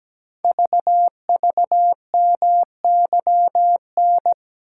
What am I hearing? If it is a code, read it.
Morse code, heard as a keyed tone, about 17 wpm, 700 Hz. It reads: VVMYN
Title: Code Groups - 5 Letters - 17wpm